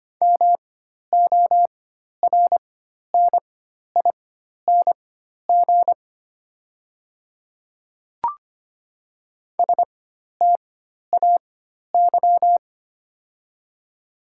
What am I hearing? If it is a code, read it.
Morse code, heard as a keyed tone, about 25 wpm, spaced as for 12 wpm, 700 Hz. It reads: MORNING STAY
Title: 4x Character Spacing - Top 500 Words - 25wpm